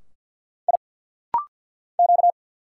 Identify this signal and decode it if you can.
Morse code, heard as a keyed tone, about 47 wpm, 690 Hz. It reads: I Q
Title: Single Letter-Number Alt - Rapid-Fire 50wpm